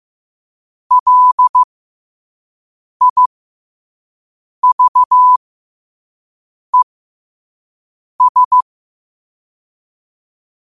Morse code, heard as a keyed tone, about 15 wpm, spaced as for 5 wpm, 1000 Hz. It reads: LIVES
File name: qotdfiles / 5wpm